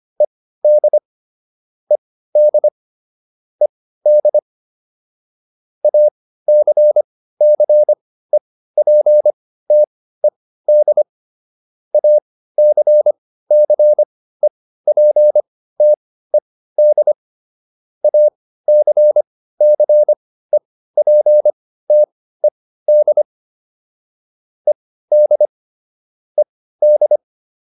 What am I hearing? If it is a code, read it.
Morse code, heard as a keyed tone, about 25 wpm, spaced as for 15 wpm, 600 Hz. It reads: ED ED ED ACCEPTED ACCEPTED ACCEPTED ED ED